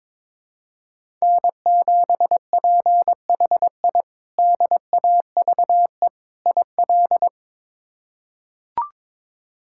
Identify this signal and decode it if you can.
Morse code, heard as a keyed tone, about 22 wpm, 700 Hz. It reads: N7PHI DAVE IL
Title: K1USN SST QSO Practice - 22wpm